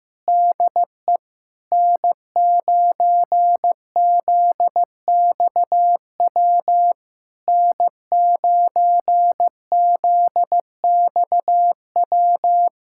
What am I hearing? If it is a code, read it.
Morse code, heard as a keyed tone, about 14 wpm, 700 Hz. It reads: DE N9ZXW N9ZXW
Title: SOTA QSO Practice 15wpm